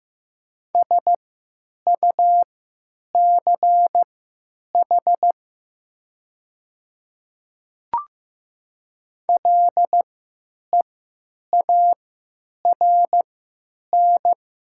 Morse code, heard as a keyed tone, about 15 wpm, spaced as for 9 wpm, 700 Hz. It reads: SUCH LEARN
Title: 3x Character Spacing - Top 300 Words - 15wpm